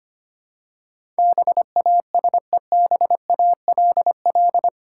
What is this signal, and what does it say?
Morse code, heard as a keyed tone, about 25 wpm, 700 Hz. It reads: BASEBALL
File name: practice / 25wpm / Difficult Words with Hints - 25wpm